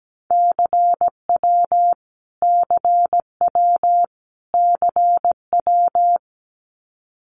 Morse code, heard as a keyed tone, about 17 wpm, 700 Hz. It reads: CW CW CW
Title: Abbreviations Repeated 3x - 17wpm